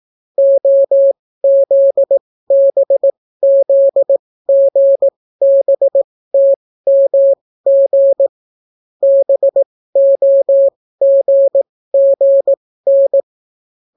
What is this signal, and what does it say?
Morse code, heard as a keyed tone, about 18 wpm, spaced as for 14 wpm, 550 Hz. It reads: OZBZGBTMG BOGGN